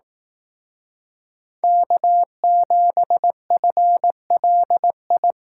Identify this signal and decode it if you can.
Morse code, heard as a keyed tone, about 18 wpm, 700 Hz. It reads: K7FLI
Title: Speed Racing - USA 1x3 Call Signs 15wpm